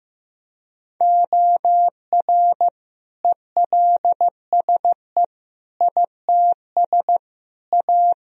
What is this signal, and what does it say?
Morse code, heard as a keyed tone, about 15 wpm, 700 Hz. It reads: OR ELSE ITS A
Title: Sentences from Top 1000 Words - Review Part 1 - 15wpm